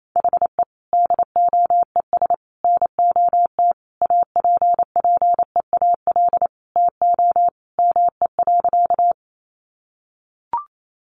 Morse code, heard as a keyed tone, about 28 wpm, 700 Hz. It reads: HE DOES NOT APPEAL TO ME.
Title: Sentences from Top 1300 Words 28wpm